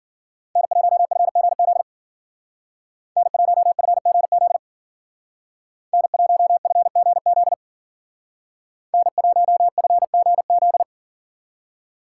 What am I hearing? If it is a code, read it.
Morse code, heard as a keyed tone, about 48 wpm, 700 Hz. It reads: N1FGZ N1FGZ N1FGZ N1FGZ
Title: Speed Racing - USA Call Signs 40wpm